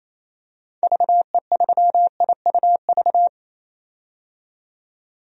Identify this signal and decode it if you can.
Morse code, heard as a keyed tone, about 28 wpm, 700 Hz. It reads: VE3IUV
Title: DXCC-100 Call Signs - 6 Characters - 28wpm